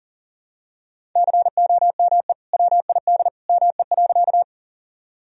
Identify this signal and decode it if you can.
Morse code, heard as a keyed tone, about 37 wpm, 700 Hz. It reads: COME WID ME.
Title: Sentences from Top 200 Words and Abbreviations 40wpm